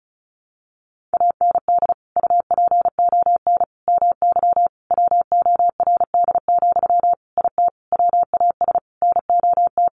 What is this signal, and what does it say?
Morse code, heard as a keyed tone, about 34 wpm, 700 Hz. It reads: AND UPON MY WORD, IT WAS NOT
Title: Sentences from Top 1100 Words 35wpm